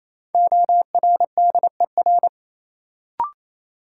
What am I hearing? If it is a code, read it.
Morse code, heard as a keyed tone, about 28 wpm, 700 Hz. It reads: ORDER
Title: Sentences from Top 400 Words - New Word Drill - 28wpm